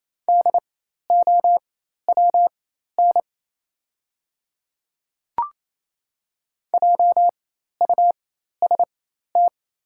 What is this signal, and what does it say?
Morse code, heard as a keyed tone, about 28 wpm, spaced as for 13 wpm, 690 Hz. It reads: DOWN JUST
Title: 4x Character Spacing - Top 200 Words - 28wpm